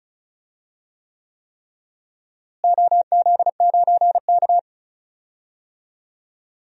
Morse code, heard as a keyed tone, about 36 wpm, 700 Hz. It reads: OZ9K
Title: DXCC-100 Call Signs - 4 Characters - Repeated 3x - 35wpm